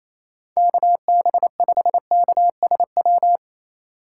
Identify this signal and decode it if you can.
Morse code, heard as a keyed tone, about 28 wpm, 700 Hz. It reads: KB5KSW
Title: USA Call Signs - 2x Character Spacing - 28wpm